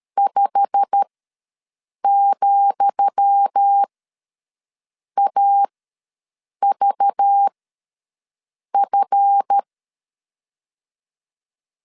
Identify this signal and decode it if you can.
Morse code, heard as a keyed tone, about 13 wpm, spaced as for 6 wpm, 800 Hz. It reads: 5,AVF
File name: LEZIONI / 10 SIGNS / 10D COMA ws 12wpm 3618